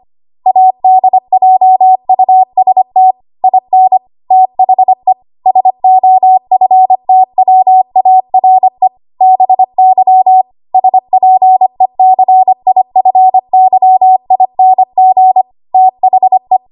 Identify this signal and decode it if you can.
Morse code, heard as a keyed tone, about 25 wpm, 750 Hz. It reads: ADJUST IN THE SOFTWARE BY SPECIFYING THE
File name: Archive / 25 WPM / 25WPM